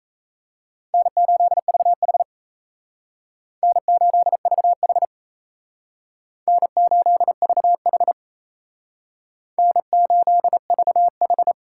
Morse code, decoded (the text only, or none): N8VH N8VH N8VH N8VH